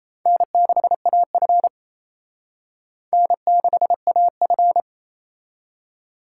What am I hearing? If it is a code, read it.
Morse code, heard as a keyed tone, about 37 wpm, 700 Hz. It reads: N6AF N6AF